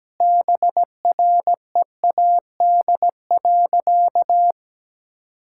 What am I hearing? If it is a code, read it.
Morse code, heard as a keyed tone, about 17 wpm, 700 Hz. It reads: BREAD.
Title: Sentences from Top 1600 Words 17wpm